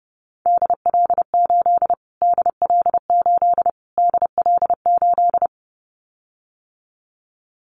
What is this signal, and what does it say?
Morse code, heard as a keyed tone, about 28 wpm, 700 Hz. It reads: DL8 DL8 DL8
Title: Callsign Prefixes for Top 25 DXCC - Repeated 3x - 30wpm